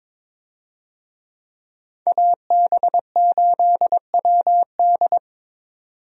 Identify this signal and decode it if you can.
Morse code, heard as a keyed tone, about 22 wpm, 700 Hz. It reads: AB8WD
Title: USA Call Signs - 5 Characters - 22wpm